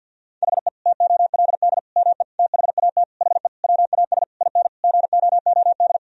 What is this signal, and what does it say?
Morse code, heard as a keyed tone, about 50 wpm, 700 Hz. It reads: HE TOLD ME THAT HE WAS IN GOOD
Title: Sentences from Top 1000 Words - Review Part 3 - 50wpm